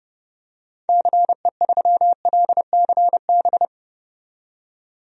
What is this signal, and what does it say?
Morse code, heard as a keyed tone, about 30 wpm, 700 Hz. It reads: CE3LCB
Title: DXCC-100 Call Signs - 6 Characters - 30wpm